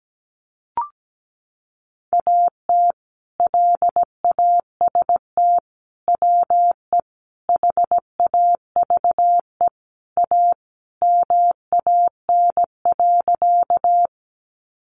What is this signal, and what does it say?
Morse code, heard as a keyed tone, about 17 wpm, 700 Hz. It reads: AT LAST WE HAVE A MAN.
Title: Sentences from Top 300 Words 17wpm